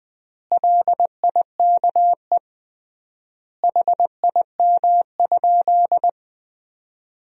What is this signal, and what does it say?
Morse code, heard as a keyed tone, about 20 wpm, 700 Hz. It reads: LIKE HIM?
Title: 3x Word-Spacing - Sentences from Top 100 Words - 20wpm